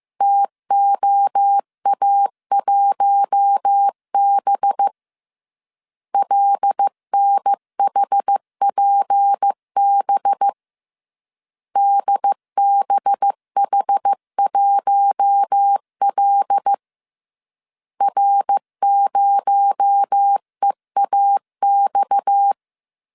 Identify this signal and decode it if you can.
Morse code, heard as a keyed tone, about 15 wpm, 800 Hz. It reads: TOA1B LNHPB DBH1L R0EAX